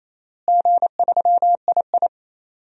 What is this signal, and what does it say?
Morse code, heard as a keyed tone, about 28 wpm, 700 Hz. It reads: G3II
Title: DXCC-100 Call Signs - 4 Characters - Repeated 2x - 28wpm